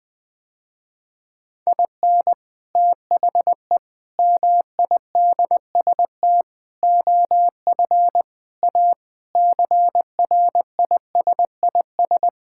Morse code, heard as a keyed tone, about 21 wpm, 700 Hz. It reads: IN THE MIDST OF A CRISIS
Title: Sentences with 5-Word Phrases - Spoken Beginning - 20wpm